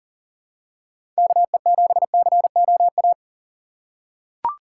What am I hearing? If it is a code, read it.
Morse code, heard as a keyed tone, about 39 wpm, 700 Hz. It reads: KE7COA
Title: USA Call Signs - 6 Characters - 3x Character Spacing - 40wpm